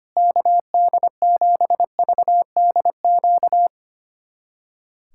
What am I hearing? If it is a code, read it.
Morse code, heard as a keyed tone, about 25 wpm, 700 Hz. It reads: KD7VDQ
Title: USA Callsign DE Callsign - 6 Characters - 25wpm